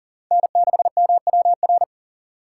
Morse code, heard as a keyed tone, about 40 wpm, 700 Hz. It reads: N6MWR
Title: USA Call Signs - 5 Characters - 6x Character Spacing - 40wpm